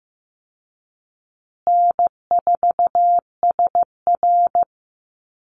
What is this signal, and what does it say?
Morse code, heard as a keyed tone, about 15 wpm, 700 Hz. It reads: N4SR
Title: Call Signs Repeated 2x - 15wpm